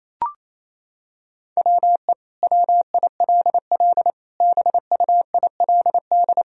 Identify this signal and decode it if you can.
Morse code, heard as a keyed tone, about 28 wpm, 700 Hz. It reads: WE WILL BUILD
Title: Sentences from Top 1000 Words - Review Part 1 - 28wpm